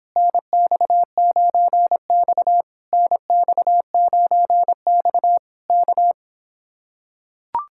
Morse code, heard as a keyed tone, about 26 wpm, 700 Hz. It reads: NX9X NX9X K